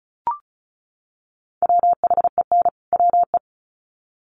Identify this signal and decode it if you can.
Morse code, heard as a keyed tone, about 34 wpm, 700 Hz. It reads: WHEN WE
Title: Sets of 2 Words 35wpm